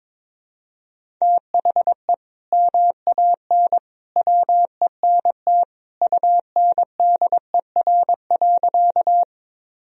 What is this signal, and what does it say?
Morse code, heard as a keyed tone, about 22 wpm, 700 Hz. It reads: THE MAN WENT UNDER.
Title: Sentences from Top 1000 Words - Review Part 1 - 22wpm